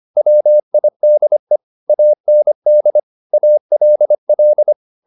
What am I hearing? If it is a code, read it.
Morse code, heard as a keyed tone, about 25 wpm, 600 Hz. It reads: WIDE AND ALL
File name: the old man and the sea part 1 25wpm